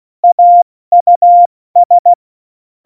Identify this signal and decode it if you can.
Morse code, heard as a keyed tone, about 16 wpm, 700 Hz. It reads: AUS
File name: Morsekurs / Witze / Witz 23. Oktober 2018, 70 F80